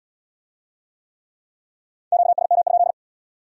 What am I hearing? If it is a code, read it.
Morse code, heard as a keyed tone, about 65 wpm, 700 Hz. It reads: QSM?